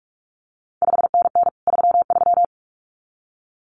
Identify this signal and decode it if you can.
Morse code, heard as a keyed tone, about 46 wpm, 700 Hz. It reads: 5NN 33
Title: CQ WW DX Contest Exchange - 45wpm